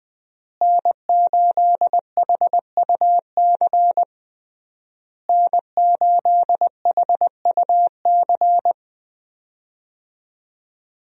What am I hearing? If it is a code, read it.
Morse code, heard as a keyed tone, about 21 wpm, 700 Hz. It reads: N8HUC N8HUC
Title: USA Call Signs - 5 Characters - Repeated 3x - 20wpm